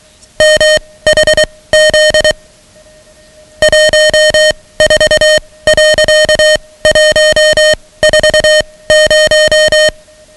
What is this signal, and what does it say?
Morse code, heard as a keyed tone, about 23 wpm, spaced as for 17 wpm, 620 Hz, with noise. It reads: MHZ 14.140
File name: ham radio / CodeQuick / Code Quick 10